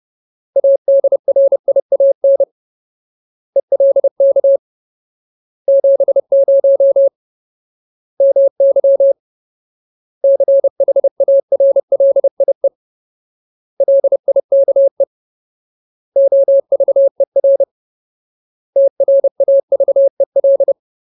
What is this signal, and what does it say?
Morse code, heard as a keyed tone, about 30 wpm, 550 Hz. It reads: ADRIAN ELK 70 MY CHARLIE LIKE OVER TRAVEL
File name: Ultimate Word List / 30WPM Ultimate